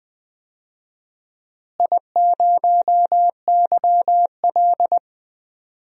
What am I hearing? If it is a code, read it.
Morse code, heard as a keyed tone, about 20 wpm, 700 Hz. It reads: I0YL
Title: DXCC-100 Call Signs - 4 Characters - Repeated 3x - 20wpm